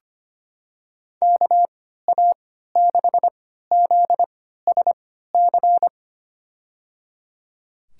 Morse code, heard as a keyed tone, about 25 wpm, spaced as for 14 wpm, 700 Hz. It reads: KA6ZSC
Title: USA Call Signs - 6 Characters - 3x Character Spacing - 25wpm